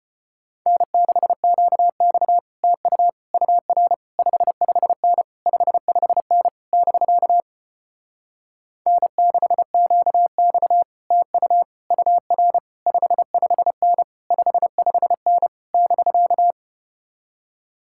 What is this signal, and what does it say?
Morse code, heard as a keyed tone, about 33 wpm, 700 Hz. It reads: N6QX TU UR 55N 55N <BK> N6QX TU UR 55N 55N <BK>